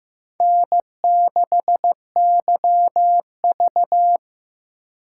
Morse code, heard as a keyed tone, about 15 wpm, 700 Hz. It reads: N6YV
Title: USA Call Signs - 4 Characters - 8x Character Spacing - 15wpm